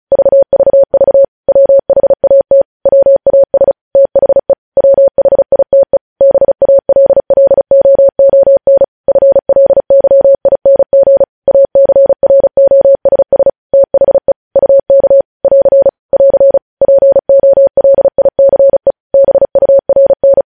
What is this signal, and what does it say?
Morse code, heard as a keyed tone, about 36 wpm, 550 Hz. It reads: VVV WHAT WAS THE WHITE BALLOON FLYING ACROSS THE UK <AR> <AR> POLICE BURN
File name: podcast / News headlines-1x-35x35